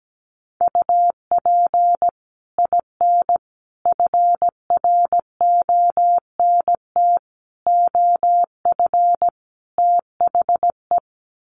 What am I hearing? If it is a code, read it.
Morse code, heard as a keyed tone, about 17 wpm, 700 Hz. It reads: UP IN FRONT OF THE
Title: Sets of 5 Words Encore2 17wpm